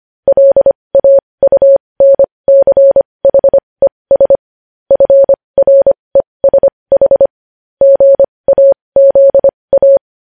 Morse code, heard as a keyed tone, about 25 wpm, spaced as for 20 wpm, 550 Hz. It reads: LAUNCHES FRESH GAZA
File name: podcast / News headlines-1x-25x20